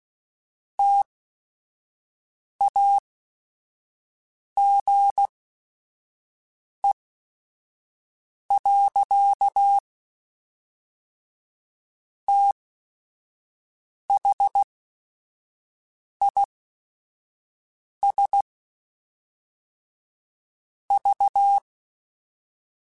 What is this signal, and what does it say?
Morse code, heard as a keyed tone, about 16 wpm, spaced as for 5 wpm, 800 Hz. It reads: TAGE. THIS V
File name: Technical Subjects / Technical Subjects-mp3s / XL @5